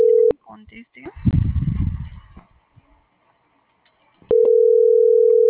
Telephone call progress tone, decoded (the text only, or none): ringback tone